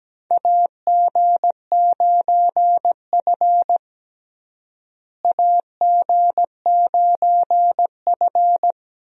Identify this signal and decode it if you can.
Morse code, heard as a keyed tone, about 17 wpm, 700 Hz. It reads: AG9F AG9F
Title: USA Call Signs - 4 Characters - Repeated 3x - 17wpm